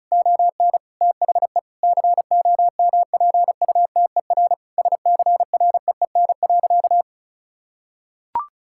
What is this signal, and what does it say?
Morse code, heard as a keyed tone, about 36 wpm, 700 Hz. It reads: ON THE COMPUTER SCREEN.
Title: Sentences from Top 1000 Words - Review Part 6 - 35wpm